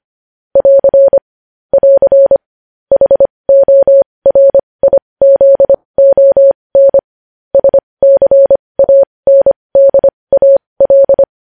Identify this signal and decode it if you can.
Morse code, heard as a keyed tone, about 25 wpm, spaced as for 20 wpm, 550 Hz. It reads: <AR> <AR> HORIZON SCANDAL
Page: news-headlines-in-morse-code-at-20-wpm